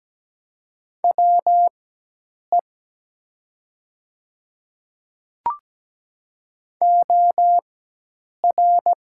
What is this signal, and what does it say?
Morse code, heard as a keyed tone, about 17 wpm, spaced as for 8 wpm, 700 Hz. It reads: WE OR
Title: 4x Character Spacing - Top 100 Words - 17wpm